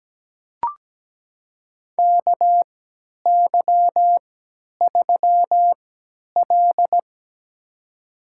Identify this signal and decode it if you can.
Morse code, heard as a keyed tone, about 17 wpm, spaced as for 10 wpm, 700 Hz. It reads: KY3L
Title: USA Call Signs - 4 Characters - 3x Character Spacing - 17wpm